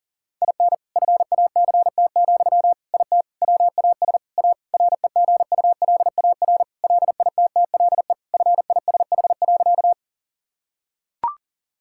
Morse code, heard as a keyed tone, about 40 wpm, 700 Hz. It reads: IN FACT, IT WAS A REGULAR LITTLE FISH.
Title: Sentences from Top 1000 Words - Review Part 6 - 40wpm